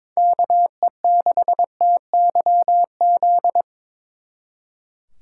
Morse code, heard as a keyed tone, about 25 wpm, 700 Hz. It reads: KE6TYZ
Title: Speed Racing - USA Call Signs 22wpm